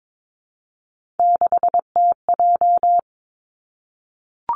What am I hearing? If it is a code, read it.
Morse code, heard as a keyed tone, about 22 wpm, 700 Hz. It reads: 6TJ
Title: Three Challenging Letter-Number Combinations 22wpm